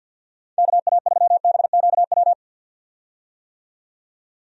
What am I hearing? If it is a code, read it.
Morse code, heard as a keyed tone, about 45 wpm, 700 Hz. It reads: KA3BQW
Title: USA Call Signs - 6 Characters - 50wpm